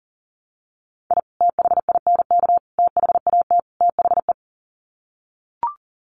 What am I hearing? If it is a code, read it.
Morse code, heard as a keyed tone, about 40 wpm, 700 Hz. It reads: I THINK THAT THE
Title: Sets of 4 Words 40wpm